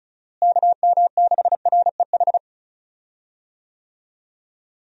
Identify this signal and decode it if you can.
Morse code, heard as a keyed tone, about 34 wpm, 700 Hz. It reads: KM6REH
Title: USA Call Signs - 6 Characters - Repeated 2x - 35wpm